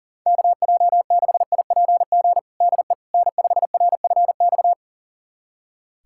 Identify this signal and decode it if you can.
Morse code, heard as a keyed tone, about 40 wpm, 700 Hz. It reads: KJ6IPG DE N5RFX
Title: USA Callsign DE Callsign 40wpm